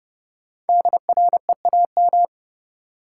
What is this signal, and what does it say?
Morse code, heard as a keyed tone, about 30 wpm, 700 Hz. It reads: DREAM
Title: Sentences from Top 1500 Words - New Word Drill Alt - 30wpm